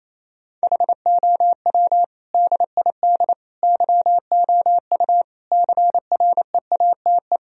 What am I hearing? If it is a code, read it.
Morse code, heard as a keyed tone, about 28 wpm, 700 Hz. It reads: HOW DID YOU CREATE
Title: Sentences from Top 1000 Words - Review Part 3 - 28wpm